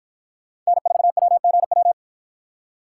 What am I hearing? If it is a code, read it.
Morse code, heard as a keyed tone, about 53 wpm, 700 Hz. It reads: N4WGW